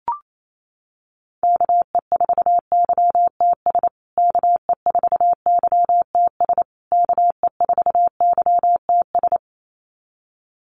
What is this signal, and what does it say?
Morse code, heard as a keyed tone, about 28 wpm, 700 Hz. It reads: KE4YTS KE4YTS KE4YTS
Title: USA Call Signs Repeated 3x 28wpm